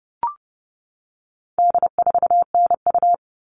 Morse code, heard as a keyed tone, about 30 wpm, 700 Hz. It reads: D4NU
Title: All Cut Numbers - 4-Digit 30wpm